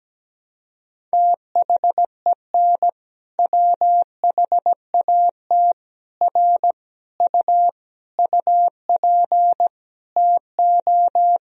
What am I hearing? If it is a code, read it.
Morse code, heard as a keyed tone, about 17 wpm, 700 Hz. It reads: THEN WHAT R U UP TO
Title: Sentences from Top 100 Words and Abbreviations 17wpm